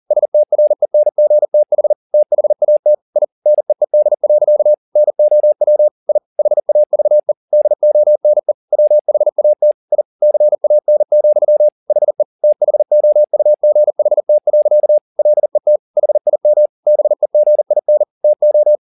Morse code, heard as a keyed tone, about 40 wpm, 600 Hz. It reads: STRENGTH THAT I NEED. NOW I HAVE DONE WHAT I CAN, HE THOUGHT. LET HIM BEGIN TO
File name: the old man and the sea 2 40wpm